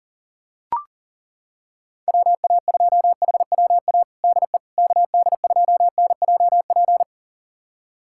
Morse code, heard as a keyed tone, about 40 wpm, 700 Hz. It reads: WA2HWA DE KD2NJP
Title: USA Callsign DE Callsign - 6 Characters - 40wpm